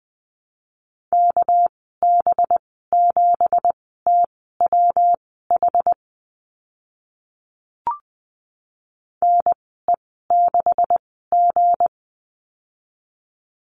Morse code, heard as a keyed tone, about 20 wpm, spaced as for 14 wpm, 700 Hz. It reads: KB7TWH NE6G
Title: 245 Lesson 39g - Letter z - Call Signs 20wpm